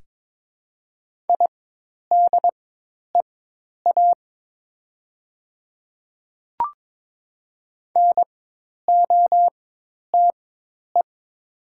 Morse code, heard as a keyed tone, about 22 wpm, spaced as for 10 wpm, 700 Hz. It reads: IDEA NOTE
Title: 4x Character Spacing - Top 500 Words - 22wpm